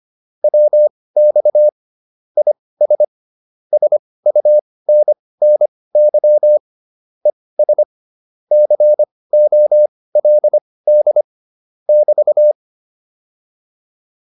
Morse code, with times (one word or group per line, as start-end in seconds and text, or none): 0.4-1.7 WX
2.4-3.0 IS
3.7-6.6 SUNNY
7.3-7.8 ES
8.5-11.2 COLD
11.9-12.5 <BT>